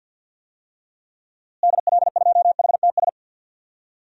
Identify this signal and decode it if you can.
Morse code, heard as a keyed tone, about 51 wpm, 700 Hz. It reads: DL2HTS
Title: DXCC-100 Call Signs - 6 Characters - Repeated 3x - 50wpm